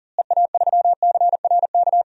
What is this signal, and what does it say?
Morse code, heard as a keyed tone, about 40 wpm, 700 Hz. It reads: EA3CRK
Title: DXCC-100 Call Signs - 6 Characters - Repeated 2x - 40wpm